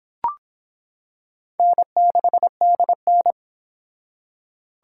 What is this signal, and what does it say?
Morse code, heard as a keyed tone, about 26 wpm, 700 Hz. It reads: N6DN